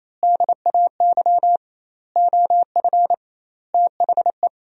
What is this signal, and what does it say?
Morse code, heard as a keyed tone, about 28 wpm, 700 Hz. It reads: DAY OF THE
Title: 2x Word-Spacing - Sentences from Top 300 Words - 28wpm